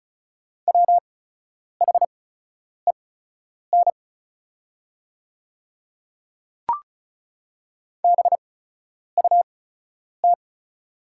Morse code, heard as a keyed tone, about 34 wpm, spaced as for 10 wpm, 700 Hz. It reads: WHEN BUT
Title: 8x Character Spacing - Top 100 Words - 35wpm